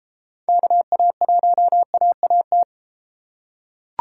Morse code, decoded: KA1AAT

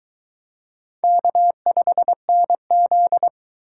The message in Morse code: K5NZ

